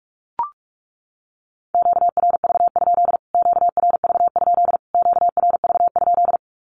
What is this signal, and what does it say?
Morse code, heard as a keyed tone, about 44 wpm, 700 Hz. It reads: QRV? QRV? QRV?
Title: Q-Codes Repeated 3x 45wpm